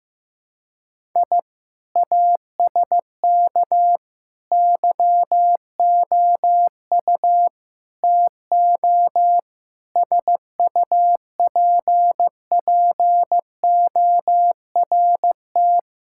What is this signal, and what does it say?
Morse code, heard as a keyed tone, about 15 wpm, 700 Hz. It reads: I ASK YOU TO SUPPORT
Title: Sentences from Top 1000 Words - Review Part 5 - 15wpm